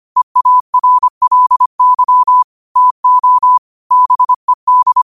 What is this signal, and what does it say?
Morse code, heard as a keyed tone, about 25 wpm, 1000 Hz. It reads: EARLY TO BED
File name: qotdfiles / 25wpm